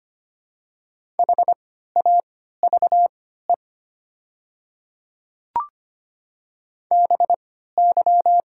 Morse code, heard as a keyed tone, about 25 wpm, spaced as for 14 wpm, 700 Hz. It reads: HAVE BY
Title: 3x Character Spacing - Top 100 Words - 25wpm